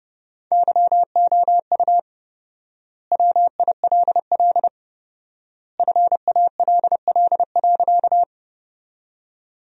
Morse code, heard as a keyed tone, about 30 wpm, 700 Hz. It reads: YOU WILL FALL.
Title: 4x Word-Spacing - Sentences from Top 500 Words - 30wpm